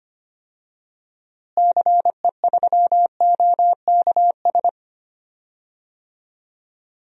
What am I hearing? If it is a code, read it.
Morse code, heard as a keyed tone, about 25 wpm, 700 Hz. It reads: CE3OKS
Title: DXCC-100 Call Signs - 6 Characters - Repeated 2x - 25wpm